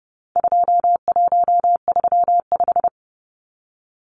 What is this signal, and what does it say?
Morse code, heard as a keyed tone, about 30 wpm, 700 Hz. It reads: 2135
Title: Numbers - 4-Digit 30wpm